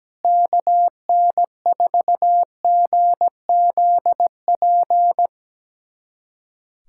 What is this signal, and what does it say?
Morse code, heard as a keyed tone, about 17 wpm, 700 Hz. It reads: KN4GZP
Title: USA Callsign DE Callsign - 6 Characters - 17wpm